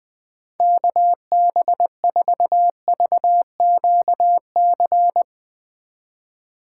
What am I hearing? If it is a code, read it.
Morse code, heard as a keyed tone, about 20 wpm, 700 Hz. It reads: KB4VQC
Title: Speed Racing - USA 2x3 Call Signs 17wpm